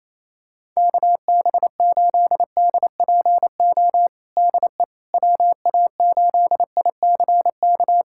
Morse code, heard as a keyed tone, about 28 wpm, 700 Hz. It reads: KB8DPO DE WA8ICK
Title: USA Callsign DE Callsign - 6 Characters - 28wpm